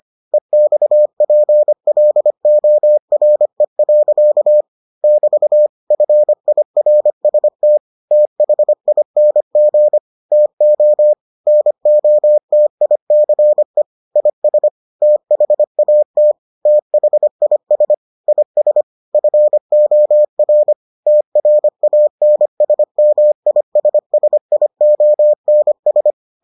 Morse code, heard as a keyed tone, about 25 wpm, 600 Hz. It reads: EXPLORE. <BT> FIRST THING TO NOTICE IS THAT THIS IS FOR TRANSMISSIONS